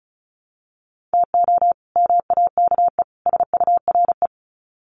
Morse code, heard as a keyed tone, about 34 wpm, 700 Hz. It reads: TO MAKE SURE
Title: Sets of 3 Words 35wpm